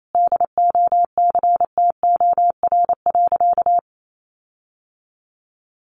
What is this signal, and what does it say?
Morse code, heard as a keyed tone, about 28 wpm, 700 Hz. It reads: DOCTOR.
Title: Sentences from Top 1500 Words 28wpm